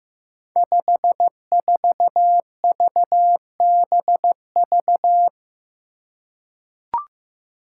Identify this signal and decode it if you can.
Morse code, heard as a keyed tone, about 15 wpm, 700 Hz. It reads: 54VBV
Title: Five Challenging Letter-Number Combinations 15wpm